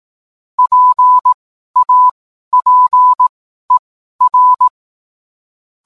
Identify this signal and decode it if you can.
Morse code, heard as a keyed tone, about 18 wpm, spaced as for 12 wpm, 1000 Hz. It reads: PAPER